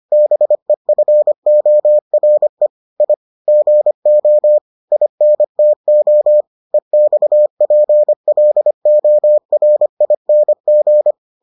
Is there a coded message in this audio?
Yes: Morse code from 0.1 to 11.1 s, heard as a keyed tone, about 25 wpm, 600 Hz. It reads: BEFORE I GO INTO EXPLORING